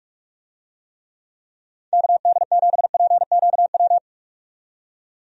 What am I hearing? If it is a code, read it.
Morse code, heard as a keyed tone, about 46 wpm, 700 Hz. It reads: KD7PQW
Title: USA Call Signs - 6 Characters - 45wpm